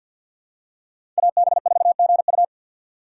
Morse code, heard as a keyed tone, about 50 wpm, 690 Hz. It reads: AB4GU
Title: USA Call Signs - 5 Characters - 2x Character Spacing - 50wpm